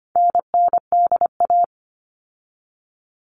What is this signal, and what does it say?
Morse code, heard as a keyed tone, about 25 wpm, 700 Hz. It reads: NNDA
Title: All Cut Numbers - 4-Digit 25wpm